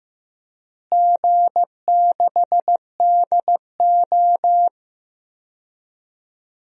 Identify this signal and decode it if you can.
Morse code, heard as a keyed tone, about 15 wpm, 700 Hz. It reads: G6DO
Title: DXCC-100 Call Signs - 4 Characters - Repeated 2x - 15wpm